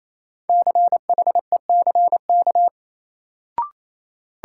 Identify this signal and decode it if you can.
Morse code, heard as a keyed tone, about 28 wpm, 700 Hz. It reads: CHECK
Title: Sentences from Top 1000 Words - Review Part 3 - New Word Drill - 28wpm